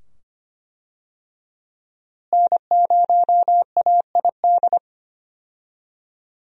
Morse code, heard as a keyed tone, about 25 wpm, 700 Hz. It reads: N0AID
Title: USA Call Signs - 5 Characters - Repeated 2x - 25wpm